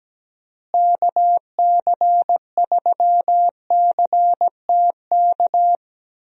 Morse code, heard as a keyed tone, about 17 wpm, 700 Hz. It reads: KC3CTK
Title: USA Call Signs - 6 Characters - Repeated 3x - 17wpm